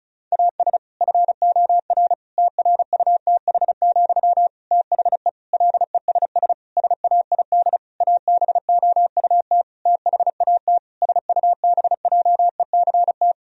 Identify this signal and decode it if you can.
Morse code, heard as a keyed tone, about 34 wpm, 700 Hz. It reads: AS FOR TRUTH, THE LESS SAID ABOUT THAT SUBJECT
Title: Sentences from Top 1000 Words - Review Part 2 - 35wpm